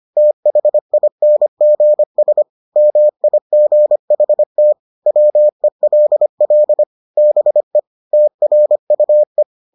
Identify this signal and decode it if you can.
Morse code, heard as a keyed tone, about 25 wpm, 600 Hz. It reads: THINGS MIGHT WELL BE TRUE